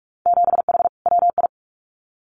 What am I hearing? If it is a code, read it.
Morse code, heard as a keyed tone, about 46 wpm, 700 Hz. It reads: 7H WI
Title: Winter Field Day Exchanges Encore 45wpm